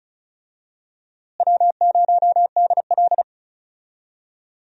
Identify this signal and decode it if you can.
Morse code, heard as a keyed tone, about 34 wpm, 700 Hz. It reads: W0DL
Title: USA Call Signs - 4 Characters - Repeated 2x - 35wpm